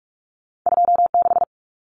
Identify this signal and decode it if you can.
Morse code, heard as a keyed tone, about 46 wpm, 700 Hz. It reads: 26
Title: Numbers - 2-Digit 45wpm